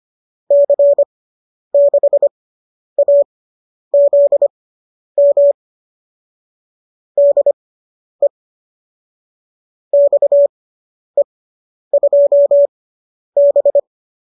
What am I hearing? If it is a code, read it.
Morse code, heard as a keyed tone, about 25 wpm, spaced as for 10 wpm, 570 Hz. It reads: C6AZM DE XE2B